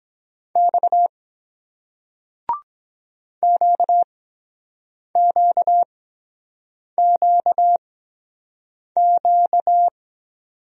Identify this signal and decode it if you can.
Morse code, heard as a keyed tone, about 23 wpm, spaced as for 7 wpm, 700 Hz. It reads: X QQQQ